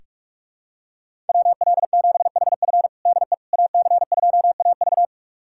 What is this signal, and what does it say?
Morse code, heard as a keyed tone, about 41 wpm, 700 Hz. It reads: WL7HF DE AC2AV
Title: USA Callsign DE Callsign - 5 Characters - 45wpm